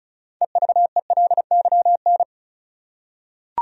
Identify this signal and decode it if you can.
Morse code, heard as a keyed tone, about 34 wpm, 700 Hz. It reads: EVELYN
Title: USA Common Names - 35wpm